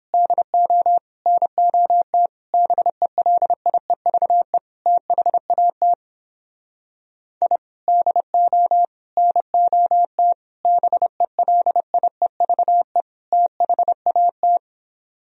Morse code, heard as a keyed tone, about 31 wpm, 700 Hz. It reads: DO NOT BELIEVE THAT I DO NOT BELIEVE THAT